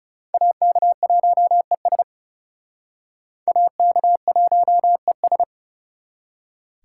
Morse code, heard as a keyed tone, about 37 wpm, 700 Hz. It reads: AK1ES AK1ES
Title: Speed Racing - USA 2x2 Call Signs 30wpm